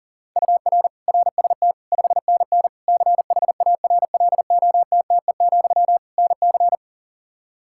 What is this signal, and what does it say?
Morse code, heard as a keyed tone, about 40 wpm, 700 Hz. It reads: UR RST 5NN CHARLOTTE, NC
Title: K1USN SST QSO Practice 40wpm